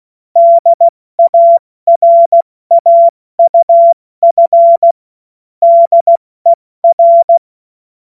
Morse code, heard as a keyed tone, about 16 wpm, 680 Hz. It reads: DARAUF DER
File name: Witz-08-Juli-2025-70-F80